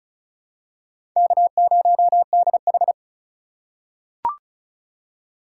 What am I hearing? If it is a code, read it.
Morse code, heard as a keyed tone, about 36 wpm, 700 Hz. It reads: K0DH
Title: USA Call Signs - 6x Character Spacing - 35wpm